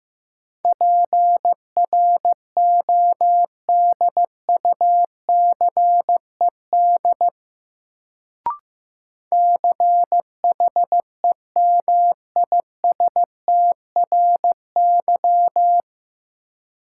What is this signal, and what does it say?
Morse code, heard as a keyed tone, about 16 wpm, 700 Hz. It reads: PRODUCED CHEMISTRY
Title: Sentences from Top 1000 Words - Review Part 6 - New Word Drill - 15wpm